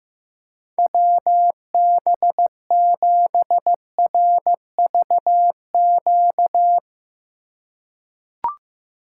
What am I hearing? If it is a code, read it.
Morse code, heard as a keyed tone, about 15 wpm, 700 Hz. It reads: WB7RVQ
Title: USA Call Signs - 4x Character Spacing - 15wpm